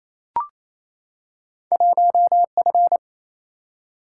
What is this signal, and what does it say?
Morse code, heard as a keyed tone, about 28 wpm, 700 Hz. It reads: 1F
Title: Single-Letter Single-Number 28wpm